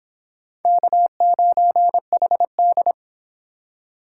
Morse code, heard as a keyed tone, about 24 wpm, 700 Hz. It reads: K9HD